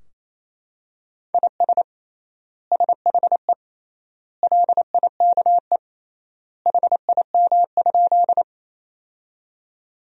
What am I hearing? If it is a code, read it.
Morse code, heard as a keyed tone, about 28 wpm, 700 Hz. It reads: IS SHE LIKE HIM?
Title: 3x Word-Spacing - Sentences from Top 100 Words - 28wpm